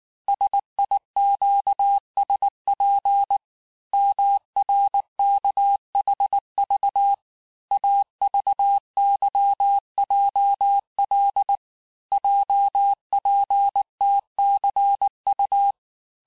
Morse code, heard as a keyed tone, about 19 wpm, 800 Hz. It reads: SIQSP MRKHV AVYJL JPTCU
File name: cours-du-lundi files / cours-1595-teste-CHIFFRE-vitesse18-MPM